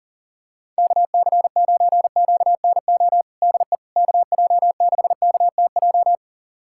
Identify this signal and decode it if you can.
Morse code, heard as a keyed tone, about 40 wpm, 700 Hz. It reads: KC9QNO DE KJ6KTJ
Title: USA Callsign DE Callsign 40wpm